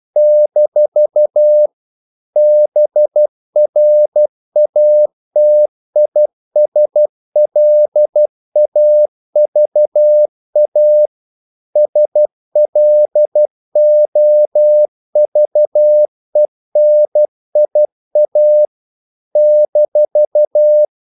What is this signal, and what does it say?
Morse code, heard as a keyed tone, about 12 wpm, 600 Hz. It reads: - BRATISLAVA SLOVENIA -
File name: CW / Practice / NationalCapitals12